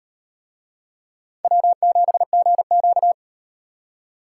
Morse code, heard as a keyed tone, about 38 wpm, 700 Hz. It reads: W7GQ